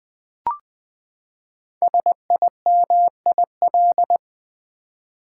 Morse code, heard as a keyed tone, about 20 wpm, 700 Hz. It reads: SIMIL
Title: Build A Word - 5 Letters At A Time - 10-Letter Words - 20wpm